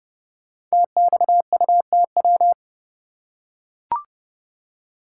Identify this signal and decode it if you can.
Morse code, heard as a keyed tone, about 30 wpm, 700 Hz. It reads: TXUTW
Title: Code Groups - 5 Letters - 30wpm